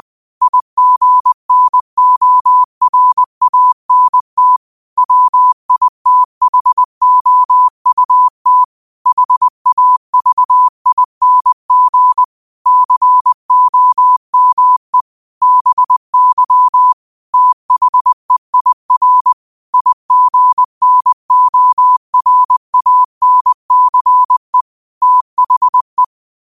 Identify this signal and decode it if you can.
Morse code, heard as a keyed tone, about 20 wpm, 1000 Hz. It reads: IGNORANT WITHOUT HAVING COME BY THEIR IGNORANCE THE
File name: qotdfiles / 20wpm